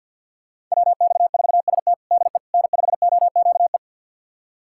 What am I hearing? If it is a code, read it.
Morse code, heard as a keyed tone, about 52 wpm, 700 Hz. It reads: WX4ST DE N5OQE
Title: USA Callsign DE Callsign 50wpm